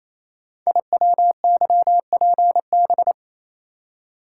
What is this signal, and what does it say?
Morse code, heard as a keyed tone, about 27 wpm, 700 Hz. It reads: IWYPB
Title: Code Groups - 5 Letters - 28wpm